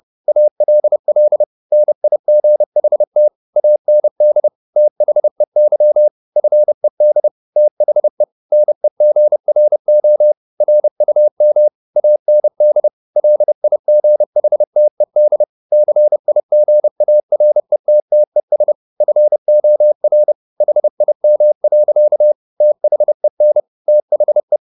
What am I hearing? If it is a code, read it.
Morse code, heard as a keyed tone, about 30 wpm, 600 Hz. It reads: ALL NIGHT AND THEY FED THE NEGRO RUM AND LIGHTED CIGARETTES FOR HIM. THEN THE